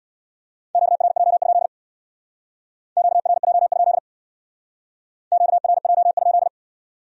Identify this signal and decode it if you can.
Morse code, heard as a keyed tone, about 61 wpm, 700 Hz. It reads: QRJ? QRJ? QRJ?
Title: Speed-Racing Q-Codes - 50wpm